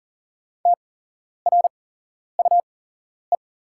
Morse code, heard as a keyed tone, about 40 wpm, spaced as for 11 wpm, 700 Hz. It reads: TRUE
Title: 8x Character Spacing - Top 500 Words - 40wpm